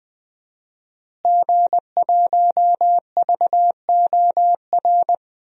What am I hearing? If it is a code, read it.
Morse code, heard as a keyed tone, about 20 wpm, 700 Hz. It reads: G1VOR
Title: DXCC-100 Call Signs - 5 Characters - Repeated 2x - 20wpm